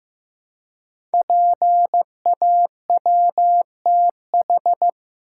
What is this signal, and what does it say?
Morse code, heard as a keyed tone, about 15 wpm, 700 Hz. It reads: PAWTH
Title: Code Groups - 5 Letters - 15wpm